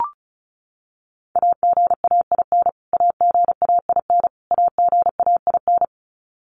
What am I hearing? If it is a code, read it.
Morse code, heard as a keyed tone, about 34 wpm, 700 Hz. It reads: AGAIN AGAIN AGAIN
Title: Top 200 Words Repeated 35wpm